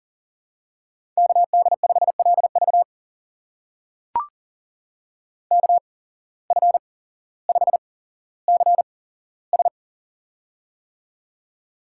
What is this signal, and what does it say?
Morse code, heard as a keyed tone, about 40 wpm, spaced as for 17 wpm, 700 Hz. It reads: KD5LV KF5CS
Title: USA Call Signs - 5 Characters - 8x Character Spacing - 40wpm